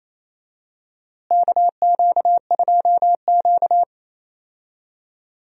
Morse code, heard as a keyed tone, about 28 wpm, 700 Hz. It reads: KQ2Q
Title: USA Call Signs - 4 Characters - Repeated 2x - 28wpm